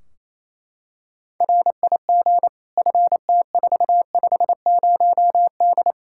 Morse code, heard as a keyed tone, about 28 wpm, 700 Hz. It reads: RIG FT450D
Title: K1USN SST QSO Practice 28wpm